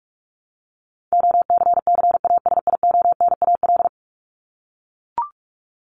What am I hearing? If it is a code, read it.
Morse code, heard as a keyed tone, about 44 wpm, 700 Hz. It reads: OCCASIONAL
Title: LY Suffix - Top 100 Words with Base Words 45wpm